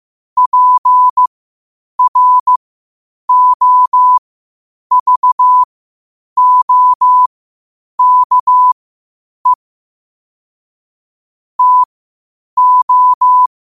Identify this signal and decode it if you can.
Morse code, heard as a keyed tone, about 15 wpm, spaced as for 8 wpm, 1000 Hz. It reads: PROVOKE TO
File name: qotdfiles / 8wpm